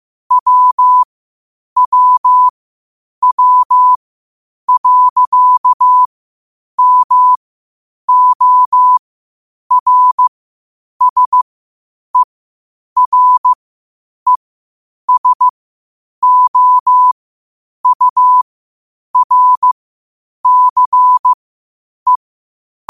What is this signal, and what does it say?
Morse code, heard as a keyed tone, about 15 wpm, spaced as for 8 wpm, 1000 Hz. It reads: WWW.MORSERESOURCE
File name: qotdfiles / 8wpm